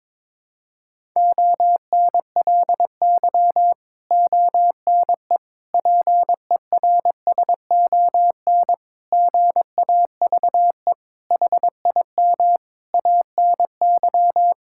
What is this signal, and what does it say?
Morse code, heard as a keyed tone, about 22 wpm, 700 Hz. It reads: ONLY ONE PERSON GAVE HIM ANY
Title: Sentences from Top 1000 Words - Review Part 6 - 22wpm